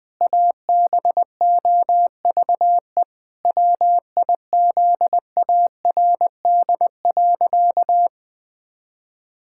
Morse code, heard as a keyed tone, about 20 wpm, 700 Hz. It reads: ABOVE WIZARD.
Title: Pangram Challenge v2 - 20wpm